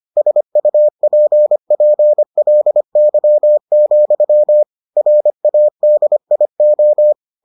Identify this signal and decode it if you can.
Morse code, heard as a keyed tone, about 25 wpm, 600 Hz. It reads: SUPPLY, RADIO